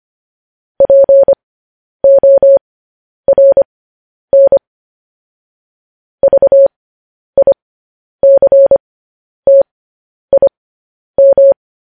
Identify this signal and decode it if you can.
Morse code, heard as a keyed tone, about 25 wpm, spaced as for 10 wpm, 550 Hz. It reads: PORN VICTIM
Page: News Headlines in Morse Code at 10 WPM